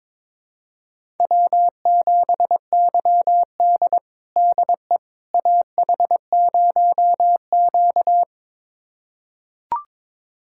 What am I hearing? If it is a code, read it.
Morse code, heard as a keyed tone, about 21 wpm, 700 Hz. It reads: W7YD DE AH0Q
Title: USA Callsign DE Callsign - 4 Characters - 22wpm